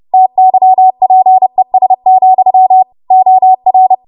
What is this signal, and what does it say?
Morse code, heard as a keyed tone, about 30 wpm, 750 Hz. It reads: TYPES, OR
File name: Morse / 30WPM